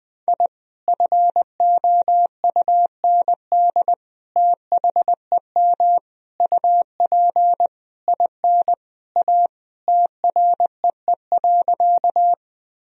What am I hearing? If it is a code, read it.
Morse code, heard as a keyed tone, about 20 wpm, 700 Hz. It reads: I FOUND THEM UP IN A TREE.
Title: Sentences from Top 1000 Words - Review Part 1 - 20wpm